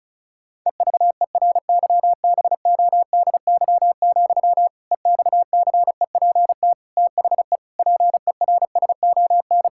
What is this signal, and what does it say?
Morse code, heard as a keyed tone, about 36 wpm, 700 Hz. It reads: EVERYBODY, EXCEPT THE PERSON
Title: Sentences from Top 1600 Words 35wpm